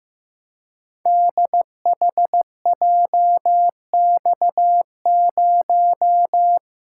Morse code, heard as a keyed tone, about 15 wpm, 700 Hz. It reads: DHJX0
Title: Code Groups - 5 Letter:Numbers - 15wpm